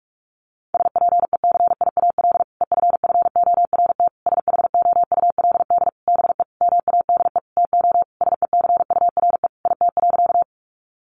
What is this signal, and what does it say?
Morse code, heard as a keyed tone, about 46 wpm, 700 Hz. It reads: SPECIAL EFFORT SHOULD BE MADE TO SECURE IT.
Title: Sentences from Top 1200 Words 45wpm